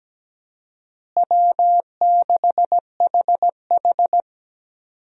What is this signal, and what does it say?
Morse code, heard as a keyed tone, about 17 wpm, 700 Hz. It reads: W6HH
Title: USA Call Signs - 4 Characters - Repeated 2x - 17wpm